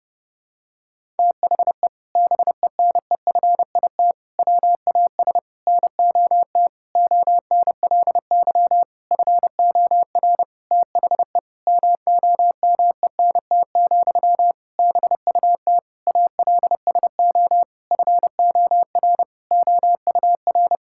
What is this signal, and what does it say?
Morse code, heard as a keyed tone, about 32 wpm, 700 Hz. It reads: THE BENEFIT WAS NOT ONLY FOR THE MOMENT, BUT ALSO FOR OUR
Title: Sentences from Top 1000 Words - Review Part 6 - 30wpm